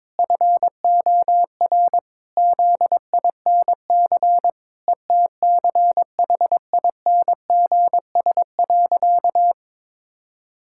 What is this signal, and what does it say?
Morse code, heard as a keyed tone, about 22 wpm, 700 Hz. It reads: FOR ZINC ETCHINGS.
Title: Pangram Challenge v2 - 22wpm